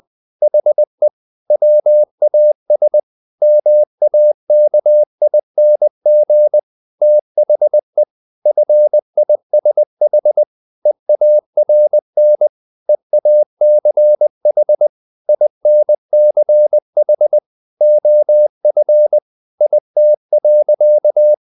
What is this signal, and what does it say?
Morse code, heard as a keyed tone, about 20 wpm, 600 Hz. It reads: HE WAS MAKING THE FISH EARN EACH INCH OF IT.
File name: the old man and the sea part 2 20wpm